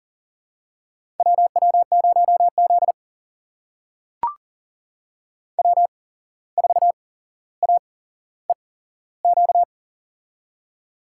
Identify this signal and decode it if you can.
Morse code, heard as a keyed tone, about 40 wpm, spaced as for 16 wpm, 700 Hz. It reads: WW0Z W4AEQ
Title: USA Call Signs - 8x Character Spacing - 40wpm